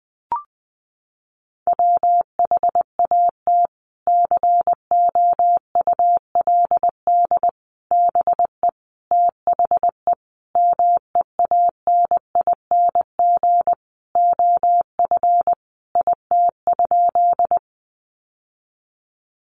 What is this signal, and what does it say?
Morse code, heard as a keyed tone, about 20 wpm, 700 Hz. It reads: WHAT COULD BE THE MEANING OF IT?
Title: Sentences from Top 900 Words 20wpm